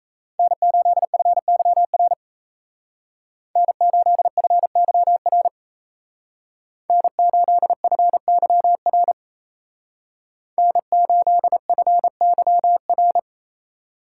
Morse code, heard as a keyed tone, about 36 wpm, 700 Hz. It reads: N8FYR N8FYR N8FYR N8FYR